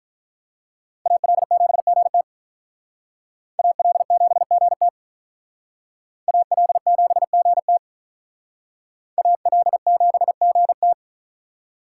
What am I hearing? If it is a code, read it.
Morse code, heard as a keyed tone, about 44 wpm, 700 Hz. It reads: AL7GT AL7GT AL7GT AL7GT
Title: Speed Racing - USA 2x2 Call Signs 35wpm